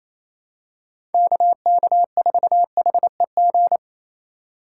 Morse code, heard as a keyed tone, about 28 wpm, 700 Hz. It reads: KK4HEG
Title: K1USN SST QSO Practice - 28wpm